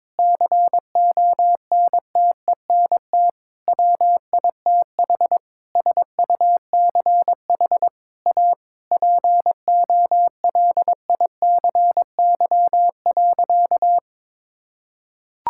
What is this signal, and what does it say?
Morse code, heard as a keyed tone, about 22 wpm, 700 Hz. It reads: CONTENT WITH SUCH A POLICY.
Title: Sentences from Top 1000 Words - Review Part 6 - 22wpm